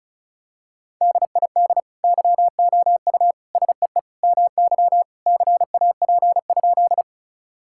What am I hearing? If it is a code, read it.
Morse code, heard as a keyed tone, about 36 wpm, 700 Hz. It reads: DID YOU SEE MY CAP?
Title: Sentences from Top 1600 Words 35wpm